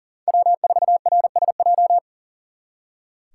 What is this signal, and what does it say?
Morse code, heard as a keyed tone, about 52 wpm, 700 Hz. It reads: W4RSJ